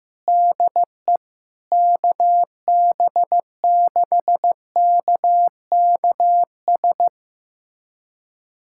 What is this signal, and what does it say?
Morse code, heard as a keyed tone, about 15 wpm, 700 Hz. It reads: DE KB6KKS
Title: USA Callsign DE Callsign - 6 Characters - 15wpm